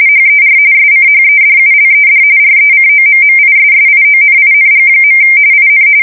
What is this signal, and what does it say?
RTTY, about 45.45 baud. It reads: HER SCHEDULED RECOMMISSIONING. IN
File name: fsk170 / USS-Missouri